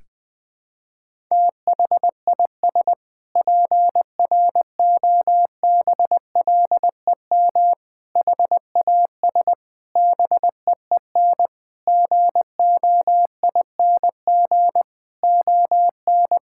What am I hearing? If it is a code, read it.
Morse code, heard as a keyed tone, about 20 wpm, 700 Hz. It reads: THIS PROBLEM HAS BEEN GOING ON
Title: Sentences with 8-Word Phrases - Spoken Beginning - 20wpm